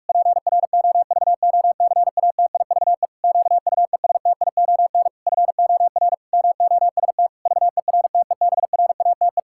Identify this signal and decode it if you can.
Morse code, heard as a keyed tone, about 41 wpm, 700 Hz. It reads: PROVOCATIVE QUESTION FOR MOST VERTEBRATE
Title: Pangram Challenge v2 - 45wpm